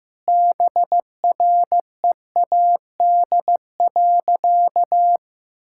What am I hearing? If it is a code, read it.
Morse code, heard as a keyed tone, about 15 wpm, 700 Hz. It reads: BREAD.
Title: Sentences from Top 1600 Words 15wpm